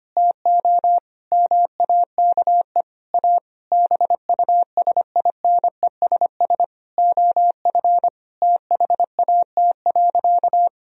Morse code, heard as a keyed tone, about 25 wpm, 700 Hz. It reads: TO MAKE A BUSINESS OF THAT.
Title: Sentences from Top 1000 Words - Review Part 4 - 25wpm